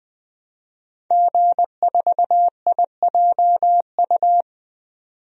Morse code, heard as a keyed tone, about 20 wpm, 700 Hz. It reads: G4IJU
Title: DXCC-100 Call Signs - 5 Characters - 20wpm